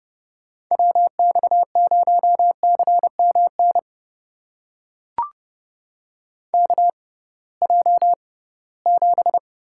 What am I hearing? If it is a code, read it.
Morse code, heard as a keyed tone, about 30 wpm, spaced as for 19 wpm, 700 Hz. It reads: WX0CMN KJ7
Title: USA Call Signs - 6 Characters - 6x Character Spacing - 30wpm